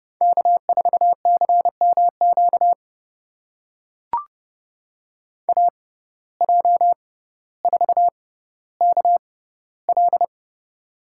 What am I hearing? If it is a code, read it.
Morse code, heard as a keyed tone, about 31 wpm, spaced as for 15 wpm, 700 Hz. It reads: K4CMQ AJ4KL
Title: USA Call Signs - 5 Characters - 6x Character Spacing - 30wpm